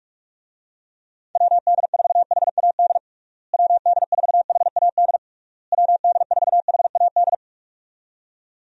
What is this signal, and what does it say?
Morse code, heard as a keyed tone, about 46 wpm, 700 Hz. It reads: WD4HAD WD4HAD WD4HAD
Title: USA Call Signs - 6 Characters - Repeated 3x - 45wpm